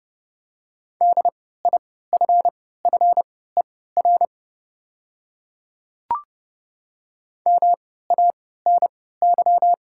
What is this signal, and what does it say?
Morse code, heard as a keyed tone, about 30 wpm, spaced as for 17 wpm, 700 Hz. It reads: DIFFER MANY
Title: 3x Character Spacing - Top 300 Words - 30wpm